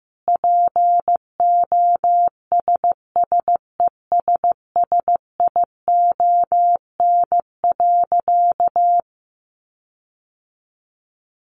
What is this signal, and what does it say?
Morse code, heard as a keyed tone, about 15 wpm, 700 Hz. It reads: POSSESSION.
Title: Sentences from Top 1400 Words 15wpm